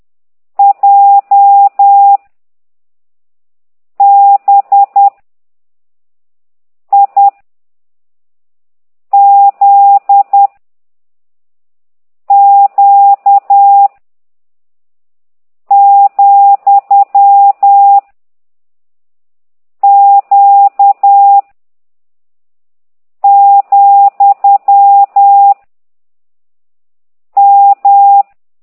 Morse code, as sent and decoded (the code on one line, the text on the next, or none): .--- / -... / .. / --.. / --.- / --..-- / --.- / --..-- / --
J B I Z Q , Q , M